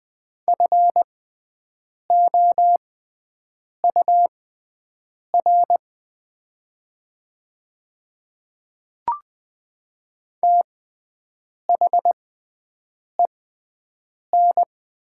Morse code, heard as a keyed tone, about 20 wpm, spaced as for 7 wpm, 700 Hz. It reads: FOUR THEN
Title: 6x Character Spacing - Top 300 Words - 20wpm